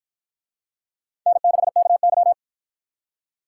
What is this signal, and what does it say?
Morse code, heard as a keyed tone, about 53 wpm, 700 Hz. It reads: N6KY